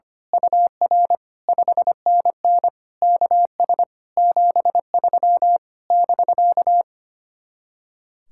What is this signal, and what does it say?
Morse code, heard as a keyed tone, about 24 wpm, 700 Hz. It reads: UR 5NN KS 73 <BK>
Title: Speed-Racing Common QSO Phrases Encore - 25wpm